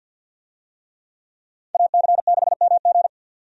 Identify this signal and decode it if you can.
Morse code, heard as a keyed tone, about 48 wpm, 700 Hz. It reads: AC6MG
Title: USA Call Signs - 5 Characters - 50wpm